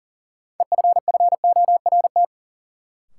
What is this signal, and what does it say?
Morse code, heard as a keyed tone, about 40 wpm, 700 Hz. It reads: EFFORT
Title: Sets of 4 Words - New Word Drill - 40wpm